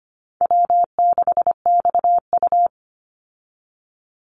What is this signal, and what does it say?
Morse code, heard as a keyed tone, about 25 wpm, 700 Hz. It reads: W6XU
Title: Easy Call Signs Repeated 3x - 25wpm